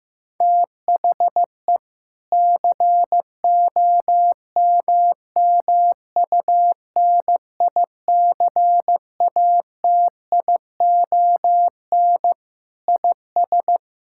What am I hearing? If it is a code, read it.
Morse code, heard as a keyed tone, about 16 wpm, 700 Hz. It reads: THE COMMUNICATION IS
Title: Sentences from Top 1000 Words - Review Part 6 - 15wpm